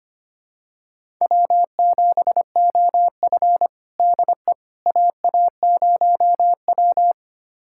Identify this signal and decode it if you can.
Morse code, heard as a keyed tone, about 25 wpm, 700 Hz. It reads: W7OF DE AA0W
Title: USA Callsign DE Callsign - 4 Characters - 25wpm